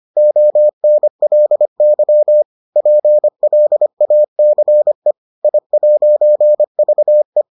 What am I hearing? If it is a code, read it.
Morse code, heard as a keyed tone, about 25 wpm, 600 Hz. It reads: ONLY PLACE I'VE